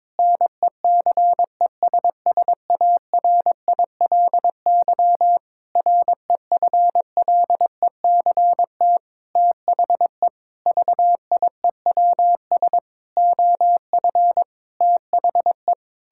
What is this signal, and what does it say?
Morse code, heard as a keyed tone, about 22 wpm, 700 Hz. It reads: NECESSARILY REFLECT THE VIEWS OF THE
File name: practice / 22wpm / Sentences with 8-Word Phrases - 22wpm